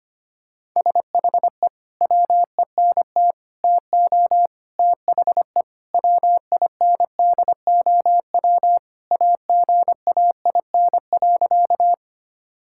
Morse code, heard as a keyed tone, about 25 wpm, 700 Hz. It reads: SHE WENT TO THE WINDOW AGAIN.
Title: Sentences from Top 1600 Words 25wpm